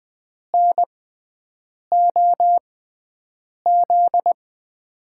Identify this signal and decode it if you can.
Morse code, heard as a keyed tone, about 20 wpm, spaced as for 7 wpm, 700 Hz. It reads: NOZ
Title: USA Call Signs - 6 Characters - 6x Character Spacing - 20wpm